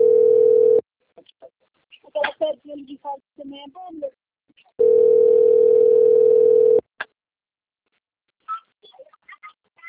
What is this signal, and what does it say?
Ringback tone (North America), 440 + 480 Hz, with speech.
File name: out- - - - - .4224